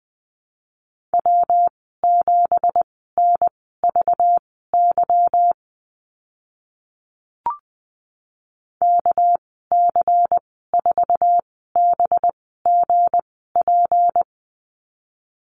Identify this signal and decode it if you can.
Morse code, heard as a keyed tone, about 20 wpm, spaced as for 15 wpm, 700 Hz. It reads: W7NVY KC4BGP
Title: 231 Lesson 37g - Letter x - Call Signs 20wpm